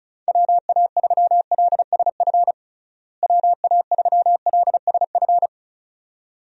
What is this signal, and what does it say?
Morse code, heard as a keyed tone, about 34 wpm, 700 Hz. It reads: WA3LSF WA3LSF
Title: USA Call Signs - 6 Characters - Repeated 2x - 35wpm